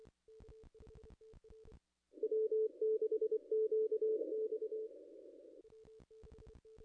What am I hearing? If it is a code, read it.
Morse code, heard as a keyed tone, about 26 wpm, 440 Hz. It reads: W6QX
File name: contests / ss-cw-k5tr-op-n5rz -0200